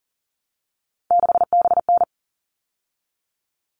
Morse code, heard as a keyed tone, about 40 wpm, 700 Hz. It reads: 6BN
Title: All Cut Numbers - 3-Digit 40wpm